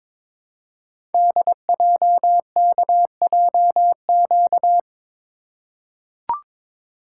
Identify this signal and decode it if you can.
Morse code, heard as a keyed tone, about 22 wpm, 700 Hz. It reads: DJKJQ
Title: Code Groups - 5 Letters - 22wpm